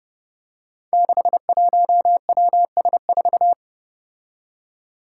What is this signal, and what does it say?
Morse code, heard as a keyed tone, about 30 wpm, 700 Hz. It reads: 61WS4
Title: Five Challenging Letter-Number Combinations 30wpm